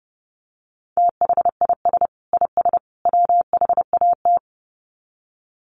Morse code, heard as a keyed tone, about 30 wpm, 700 Hz. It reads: THIS IS WHAT
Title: Sets of 3 Words 30wpm